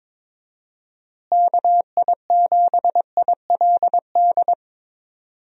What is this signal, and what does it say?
Morse code, heard as a keyed tone, about 22 wpm, 700 Hz. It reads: KI7ILD
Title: USA Call Signs - 6 Characters - Repeated 3x - 22wpm